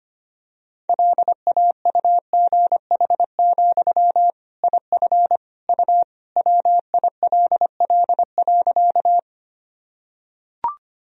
Morse code, heard as a keyed tone, about 25 wpm, 700 Hz. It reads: LAUGH, IF U WILL.
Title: Sentences from Top 500 Words and Abbreviations 25wpm